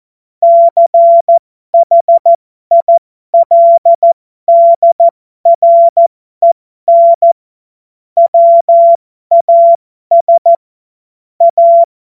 Morse code, heard as a keyed tone, about 14 wpm, spaced as for 12 wpm, 680 Hz. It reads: CHILDREN WAS A
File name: Morsekurs / Witze